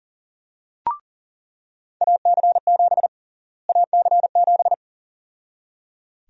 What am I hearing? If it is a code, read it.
Morse code, heard as a keyed tone, about 37 wpm, 700 Hz. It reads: AC7 AC7
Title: USA Call Sign Prefixes - Repeated 2x - 40wpm